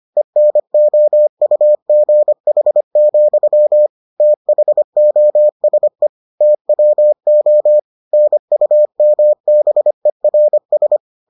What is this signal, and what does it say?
Morse code, heard as a keyed tone, about 25 wpm, 600 Hz. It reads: ENOUGH, THOSE TWO NUMBERS